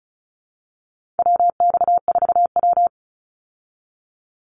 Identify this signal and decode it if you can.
Morse code, heard as a keyed tone, about 34 wpm, 700 Hz. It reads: WX4W
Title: Call Signs Repeated 3x - 35wpm